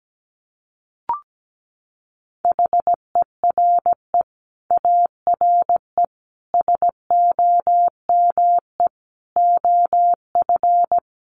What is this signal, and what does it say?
Morse code, heard as a keyed tone, about 17 wpm, 700 Hz. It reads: HERE ARE SOME OF
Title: Sets of 5 Words Encore 17wpm